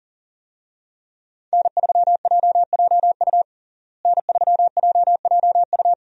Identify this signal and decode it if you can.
Morse code, heard as a keyed tone, about 40 wpm, 700 Hz. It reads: N3JJU N3JJU
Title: USA Call Signs - 5 Characters - Repeated 2x - 40wpm